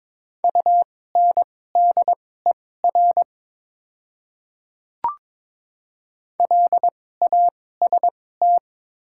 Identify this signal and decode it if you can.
Morse code, heard as a keyed tone, about 22 wpm, spaced as for 16 wpm, 700 Hz. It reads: UNDER LAST
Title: 2x Character Spacing - Top 300 Words - 22wpm